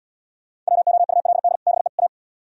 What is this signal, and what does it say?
Morse code, heard as a keyed tone, about 69 wpm, 700 Hz. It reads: WOULD BE A